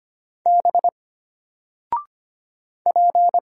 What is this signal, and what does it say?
Morse code, heard as a keyed tone, about 25 wpm, 700 Hz. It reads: B P
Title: Single Letter-Number Alt 25wpm